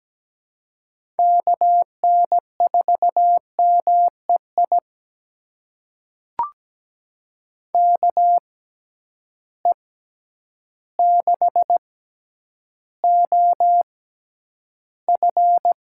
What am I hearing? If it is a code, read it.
Morse code, heard as a keyed tone, about 17 wpm, spaced as for 9 wpm, 700 Hz. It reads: KN4MEI KE6OF
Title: USA Call Signs - 6 Characters - 6x Character Spacing - 17wpm